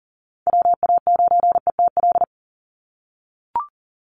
Morse code, heard as a keyed tone, about 40 wpm, 700 Hz. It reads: WA9ETL
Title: USA Call Signs 40wpm